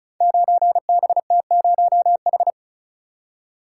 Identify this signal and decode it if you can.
Morse code, heard as a keyed tone, about 36 wpm, 700 Hz. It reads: 9BT0H
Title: Code Groups - 5 Letter:Numbers - 35wpm